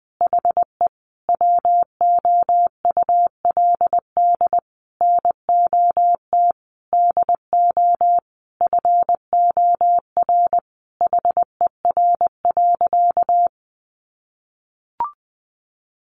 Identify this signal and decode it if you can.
Morse code, heard as a keyed tone, about 20 wpm, 700 Hz. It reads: HE WOULD NOT DO FOR HER.
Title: Sentences from Top 600 Words 20wpm